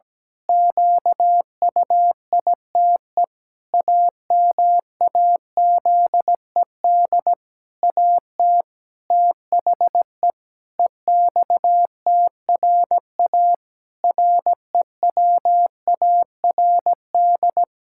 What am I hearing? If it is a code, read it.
Morse code, heard as a keyed tone, about 17 wpm, 700 Hz. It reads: QUITE AMAZED AT THE EXTRA REWARD
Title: Pangram Challenge v2 - 17wpm